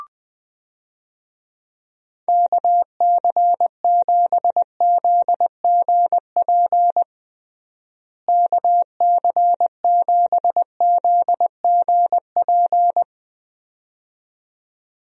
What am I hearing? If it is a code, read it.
Morse code, heard as a keyed tone, about 20 wpm, 700 Hz. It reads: KC7ZGP KC7ZGP
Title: USA Call Signs - 6 Characters - Repeated 2x - 20wpm